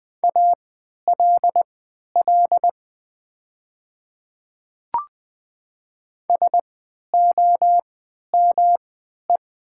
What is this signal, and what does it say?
Morse code, heard as a keyed tone, about 20 wpm, spaced as for 11 wpm, 700 Hz. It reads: ALL SOME
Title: 3x Character Spacing - Top 100 Words - 20wpm